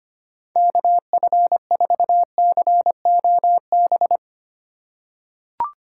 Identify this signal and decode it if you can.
Morse code, heard as a keyed tone, about 25 wpm, 700 Hz. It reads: KF4COB
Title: USA Call Signs - 6 Characters - 3x Character Spacing - 25wpm